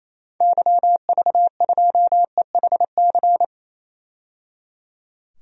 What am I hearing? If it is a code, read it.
Morse code, heard as a keyed tone, about 28 wpm, 700 Hz. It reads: YV2EHC
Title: DXCC-100 Call Signs - 6 Characters - 28wpm